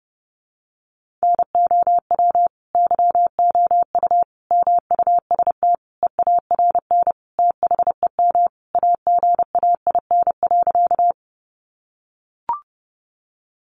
Morse code, heard as a keyed tone, about 30 wpm, 700 Hz. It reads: NOW YOU MUST EARN THEM AGAIN.
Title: Sentences from Top 1500 Words 30wpm